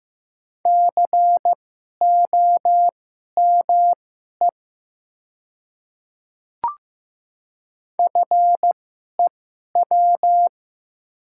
Morse code, heard as a keyed tone, about 15 wpm, spaced as for 11 wpm, 700 Hz. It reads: COME FEW
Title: 2x Character Spacing - Top 300 Words - 15wpm